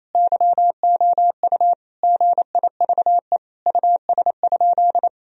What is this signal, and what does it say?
Morse code, heard as a keyed tone, about 28 wpm, 700 Hz. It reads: YOU GIVE US?
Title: Sentences from Top 1000 Words - Review Part 4 - 28wpm